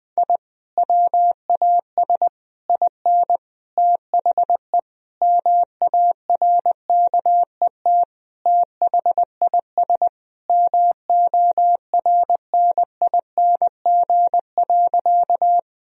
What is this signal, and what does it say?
Morse code, heard as a keyed tone, about 20 wpm, 700 Hz. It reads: I WAS IN THE MARKET THIS MORNING.
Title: Sentences from Top 1000 Words - Review Part 3 - 20wpm